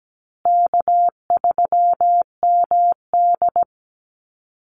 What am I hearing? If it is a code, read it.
Morse code, heard as a keyed tone, about 17 wpm, 700 Hz. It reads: K3MD
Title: Easy Call Signs Repeated 3x - 17wpm